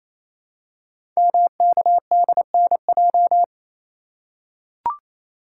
Morse code, heard as a keyed tone, about 29 wpm, 700 Hz. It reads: MKDNJ
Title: Code Groups - 5 Letters - 28wpm